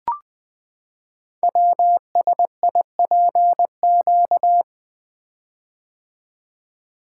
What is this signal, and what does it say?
Morse code, heard as a keyed tone, about 20 wpm, 700 Hz. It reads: WSIPQ
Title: Code Groups - 5 Letters - 20wpm